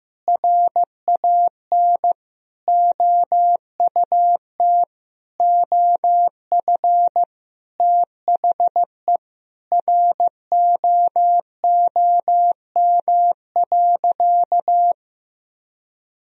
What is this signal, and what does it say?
Morse code, heard as a keyed tone, about 15 wpm, 700 Hz. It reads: RAN OUT OF THE ROOM.
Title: Sentences from Top 500 Words and Abbreviations 15wpm